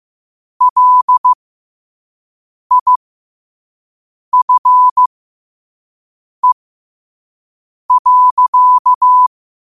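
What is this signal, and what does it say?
Morse code, heard as a keyed tone, about 15 wpm, spaced as for 5 wpm, 1000 Hz. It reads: LIFE.
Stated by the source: Morse code quote of the day at 5 wpm.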